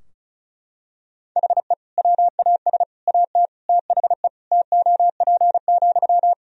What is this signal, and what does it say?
Morse code, heard as a keyed tone, about 36 wpm, 700 Hz. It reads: HE WAS AT THE TOP,
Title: Sentences from Top 1000 Words - Review Part 2 - 35wpm